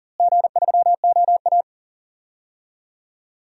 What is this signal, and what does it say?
Morse code, heard as a keyed tone, about 40 wpm, 700 Hz. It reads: G3OA
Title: DXCC-100 Call Signs - 4 Characters - Repeated 2x - 40wpm